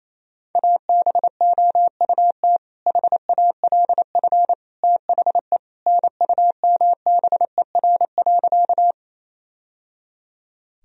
Morse code, heard as a keyed tone, about 28 wpm, 700 Hz. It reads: ABOUT HALF THE NUMBER.
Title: Sentences from Top 1000 Words - Review Part 1 - 28wpm